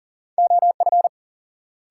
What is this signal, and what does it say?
Morse code, heard as a keyed tone, about 37 wpm, 700 Hz. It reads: OF
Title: Sentences from Top 100 Words - New Word Drill - 40wpm